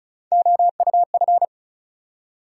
Morse code, heard as a keyed tone, about 36 wpm, 700 Hz. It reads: OUF
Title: USA Call Sign Postfixes - 3 Letters 35wpm